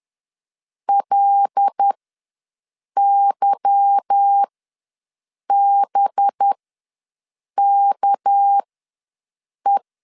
Morse code, heard as a keyed tone, about 10 wpm, spaced as for 6 wpm, 800 Hz. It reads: LYBKE
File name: LEZIONI / 9 2867 ws 10wpm 4215